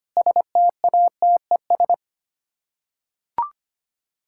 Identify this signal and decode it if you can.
Morse code, heard as a keyed tone, about 25 wpm, 700 Hz. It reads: STATES
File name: Sets of 9 Words - 25wpm